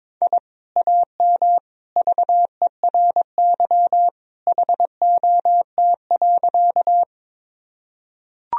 Morse code, heard as a keyed tone, about 22 wpm, 700 Hz. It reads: I AM VERY HOT.
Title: Sentences from Top 1000 Words - Review Part 3 - 22wpm